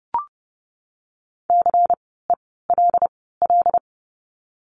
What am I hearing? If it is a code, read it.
Morse code, heard as a keyed tone, about 30 wpm, 700 Hz. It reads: C E L L
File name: Lesson 16f - Letter c - Words 30wpm